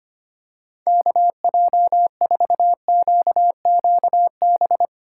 Morse code, heard as a keyed tone, about 25 wpm, 700 Hz. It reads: KJ4QQB
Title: USA Call Signs - 6 Characters - 4x Character Spacing - 25wpm